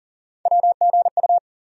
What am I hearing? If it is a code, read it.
Morse code, heard as a keyed tone, about 40 wpm, 700 Hz. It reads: WGU
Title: USA Call Sign Postfixes - 3 Letters 40wpm